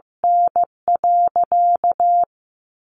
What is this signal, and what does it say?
Morse code, heard as a keyed tone, about 15 wpm, 700 Hz. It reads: N.
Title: Sentences from Top 1200 Words 15wpm